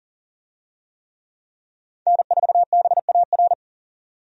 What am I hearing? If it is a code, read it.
Morse code, heard as a keyed tone, about 40 wpm, 700 Hz. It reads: N4BAR
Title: USA Call Signs - 5 Characters - 40wpm